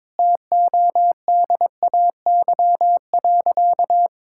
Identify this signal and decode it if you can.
Morse code, heard as a keyed tone, about 22 wpm, 700 Hz. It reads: TODAY.
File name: practice / 22wpm / Sentences with 4-Word Phrases - 22wpm